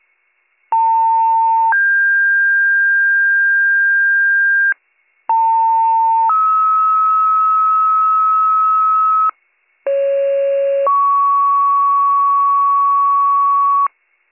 Two-tone page, heard as a one-second tone, then a three-second tone, about 900 then 1600 Hz, about 900 then 1290 Hz, about 550 then 1120 Hz, with noise. It is heard from 0.7 to 13.9 s.